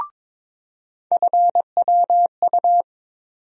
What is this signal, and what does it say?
Morse code, heard as a keyed tone, about 22 wpm, 700 Hz. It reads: FWU
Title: USA Call Sign Postfixes - 3 Letters 22wpm